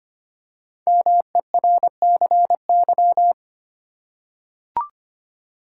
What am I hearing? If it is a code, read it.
Morse code, heard as a keyed tone, about 25 wpm, 700 Hz. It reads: MERCY
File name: 5 Letter Words - First 3 Letters Given - 25wpm